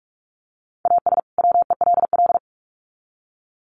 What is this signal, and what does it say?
Morse code, heard as a keyed tone, about 44 wpm, 700 Hz. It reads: AS WELL
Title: Sets of 2 Words 45wpm